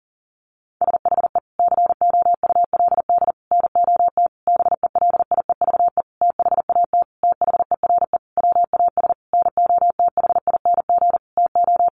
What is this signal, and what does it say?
Morse code, heard as a keyed tone, about 37 wpm, 700 Hz. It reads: SHE COULD NOT BELIEVE THAT THERE WAS NOTHING TO
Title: Sentences from Top 1000 Words 40wpm